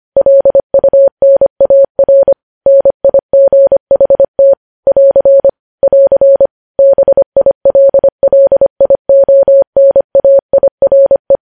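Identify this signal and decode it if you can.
Morse code, heard as a keyed tone, about 25 wpm, 550 Hz. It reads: LUNAR NIGHT <AR> <AR> BILLIONAIRE
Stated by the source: News headlines sent at 25.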